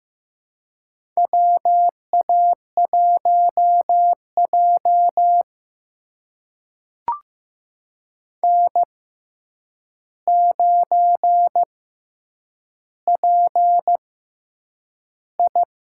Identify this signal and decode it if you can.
Morse code, heard as a keyed tone, about 15 wpm, spaced as for 8 wpm, 700 Hz. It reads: WA1J N9PI
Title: USA Call Signs - 4 Characters - 6x Character Spacing - 15wpm